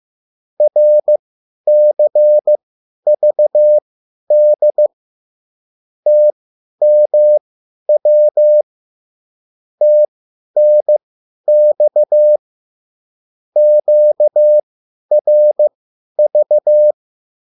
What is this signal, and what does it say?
Morse code, heard as a keyed tone, about 15 wpm, spaced as for 10 wpm, 600 Hz. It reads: RCVD TMW TNX QRV